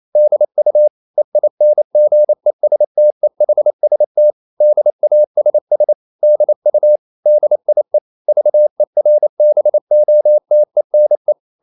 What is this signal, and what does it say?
Morse code, heard as a keyed tone, about 28 wpm, 600 Hz. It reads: DU EINGESTEHST DASS DU DIE VERBOTENE
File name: marienkind9 28wpm